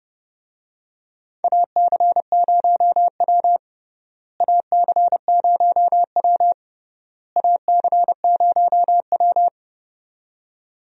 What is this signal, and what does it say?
Morse code, heard as a keyed tone, about 32 wpm, 700 Hz. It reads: AC0W AC0W AC0W
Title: USA Call Signs - 4 Characters - Repeated 3x - 30wpm